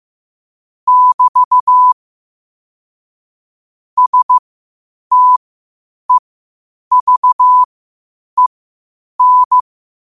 Morse code, heard as a keyed tone, about 15 wpm, spaced as for 8 wpm, 1000 Hz. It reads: <BT> STEVEN
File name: qotdfiles / 8wpm